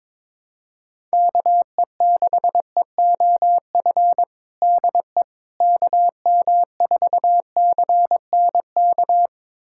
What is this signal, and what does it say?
Morse code, heard as a keyed tone, about 22 wpm, 700 Hz. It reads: KE6EOF DE KM4CNK
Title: USA Callsign DE Callsign - 6 Characters - 22wpm